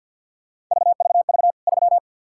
Morse code, heard as a keyed tone, about 46 wpm, 700 Hz. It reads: VVV 3
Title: K1USN SST QSO Practice 50wpm